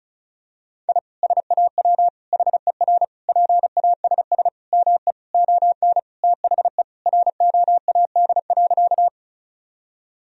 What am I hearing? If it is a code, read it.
Morse code, heard as a keyed tone, about 34 wpm, 700 Hz. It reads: I SAW HER PASS ME ON THE ROAD.
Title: Sentences from Top 1000 Words - Review Part 2 - 35wpm